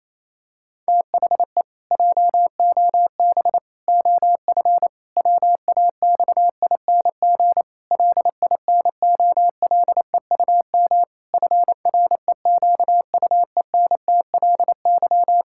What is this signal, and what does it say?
Morse code, heard as a keyed tone, about 28 wpm, 700 Hz. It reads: THE JOB OF WAXING LINOLEUM FREQUENTLY
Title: Pangram Challenge v2 - 28wpm